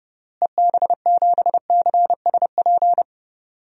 Morse code, heard as a keyed tone, about 32 wpm, 700 Hz. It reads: EB7CSP
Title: DXCC-100 Call Signs - 6 Characters - Repeated 3x - 30wpm